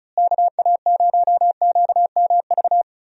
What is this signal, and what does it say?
Morse code, heard as a keyed tone, about 34 wpm, 700 Hz. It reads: KA0QMV
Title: USA Call Signs - 6 Characters - Repeated 3x - 35wpm